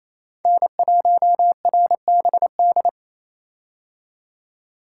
Morse code, heard as a keyed tone, about 28 wpm, 700 Hz. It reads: N1RBD
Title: K1USN SST QSO Practice - 28wpm